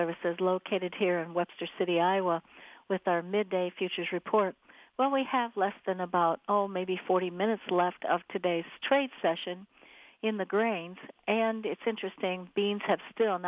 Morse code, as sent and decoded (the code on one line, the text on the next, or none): none